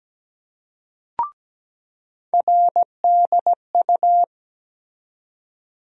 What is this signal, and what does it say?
Morse code, heard as a keyed tone, about 17 wpm, 700 Hz. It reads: RDU
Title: USA Call Sign Postfixes - 3 Letters 17wpm